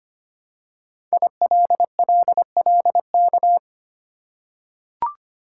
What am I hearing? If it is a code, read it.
Morse code, heard as a keyed tone, about 25 wpm, 700 Hz. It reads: ILLLK
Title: Code Groups - 5 Letters - 25wpm